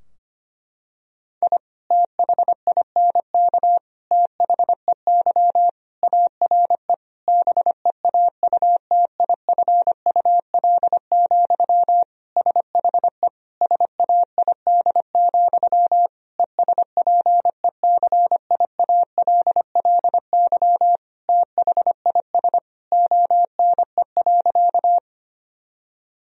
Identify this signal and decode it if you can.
Morse code, heard as a keyed tone, about 25 wpm, 700 Hz. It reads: I THINK THEY ARE BEAUTIFUL, SHE SAID, ESPECIALLY THIS ONE.
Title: Sentences from Top 1000 Words - Review Part 4 - 25wpm